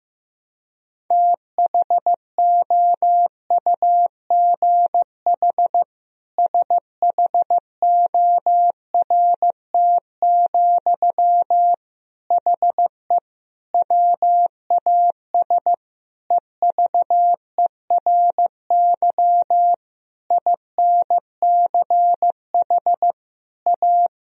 Morse code, heard as a keyed tone, about 15 wpm, 700 Hz. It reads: THOUGH SHORT, HE WAS EVERY INCH A
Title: Sentences from Top 1000 Words - Review Part 3 - 15wpm